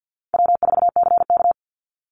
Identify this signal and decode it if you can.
Morse code, heard as a keyed tone, about 50 wpm, 700 Hz. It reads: W4CK
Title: Call Signs - 50wpm